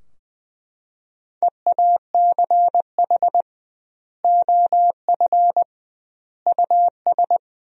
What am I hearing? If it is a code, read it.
Morse code, heard as a keyed tone, about 21 wpm, 700 Hz. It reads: EACH OF US
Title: 2x Word-Spacing - Sentences from Top 500 Words - 20wpm